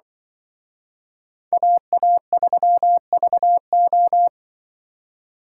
Morse code, heard as a keyed tone, about 23 wpm, 700 Hz. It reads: AA3VO
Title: Speed Racing - USA 2x2 Call Signs 20wpm